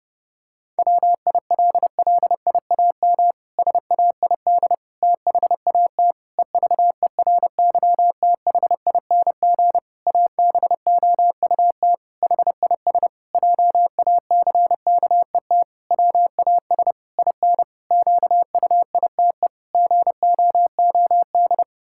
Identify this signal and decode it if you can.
Morse code, heard as a keyed tone, about 30 wpm, 700 Hz. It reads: WILLIAM SAID THAT EVERYTHING ABOUT HIS JACKET WAS IN QUITE GOOD
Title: Pangram Challenge v2 - 30wpm